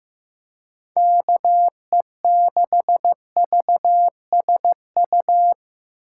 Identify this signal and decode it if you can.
Morse code, heard as a keyed tone, about 15 wpm, 700 Hz. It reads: KE6VSU
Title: USA Call Signs - 6 Characters - 4x Character Spacing - 15wpm